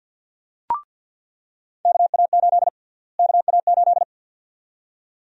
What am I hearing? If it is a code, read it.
Morse code, heard as a keyed tone, about 47 wpm, 700 Hz. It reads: KA8 KA8
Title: USA Call Sign Prefixes - Repeated 2x - 50wpm